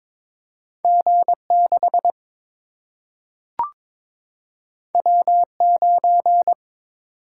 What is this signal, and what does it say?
Morse code, heard as a keyed tone, about 22 wpm, 700 Hz. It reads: G 6 W 9
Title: Single-Letter Single-Number 22wpm